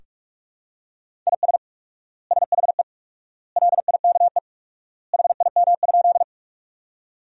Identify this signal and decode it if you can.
Morse code, heard as a keyed tone, about 42 wpm, 700 Hz. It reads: IS SHE LIKE HIM?
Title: 4x Word-Spacing - Sentences from Top 100 Words - 45wpm